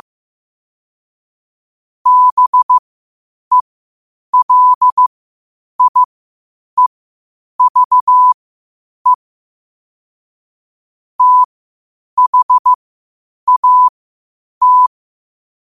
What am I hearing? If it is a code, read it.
Morse code, heard as a keyed tone, about 15 wpm, spaced as for 8 wpm, 1000 Hz. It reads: BELIEVE THAT
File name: qotdfiles / 8wpm